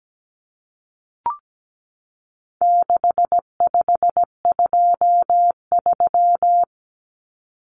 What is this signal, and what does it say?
Morse code, heard as a keyed tone, about 17 wpm, 700 Hz. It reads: 6523
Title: Numbers - 4-Digit 17wpm